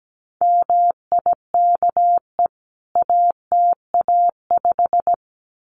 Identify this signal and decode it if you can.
Morse code, heard as a keyed tone, about 17 wpm, 700 Hz. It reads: MIKE ATA5
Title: CWT Exchange 17wpm